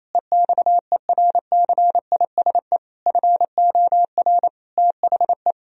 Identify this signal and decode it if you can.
Morse code, heard as a keyed tone, about 28 wpm, 700 Hz. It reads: EXERCISE FOR THE
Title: Sentences from Top 1000 Words - Review Part 4 - 28wpm